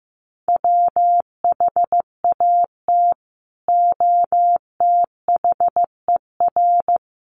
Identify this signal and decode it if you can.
Morse code, heard as a keyed tone, about 15 wpm, 700 Hz. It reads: WHAT OTHER
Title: Sentences from Top 100 Words 15wpm